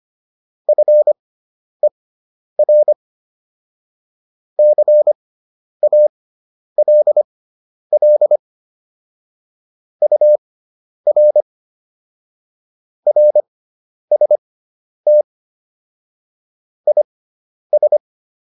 Morse code, heard as a keyed tone, about 25 wpm, spaced as for 10 wpm, 600 Hz. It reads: FER CALL UR RST IS
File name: QSO208 10